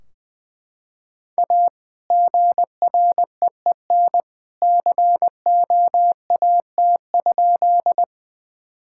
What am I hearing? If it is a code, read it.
Morse code, heard as a keyed tone, about 21 wpm, 700 Hz. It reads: A GREEN COAT?
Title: Sentences from Top 1600 Words 20wpm